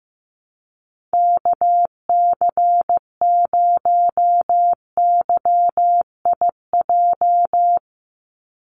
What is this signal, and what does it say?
Morse code, heard as a keyed tone, about 15 wpm, 700 Hz. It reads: KC0YIJ
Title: USA Call Signs 15wpm